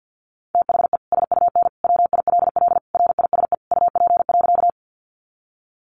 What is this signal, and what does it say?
Morse code, heard as a keyed tone, about 50 wpm, 700 Hz. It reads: THE SUN WILL RISE UP.
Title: Sentences from Top 900 Words 50wpm